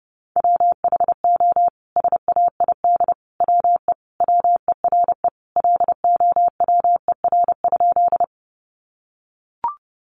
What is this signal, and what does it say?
Morse code, heard as a keyed tone, about 30 wpm, 700 Hz. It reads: WHO SAID WE WERE LOWER?
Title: Sentences from Top 700 Words 30wpm